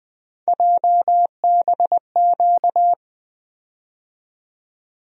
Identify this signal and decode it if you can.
Morse code, heard as a keyed tone, about 20 wpm, 700 Hz. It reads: JBQ
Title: USA Call Sign Postfixes - 3 Letters 20wpm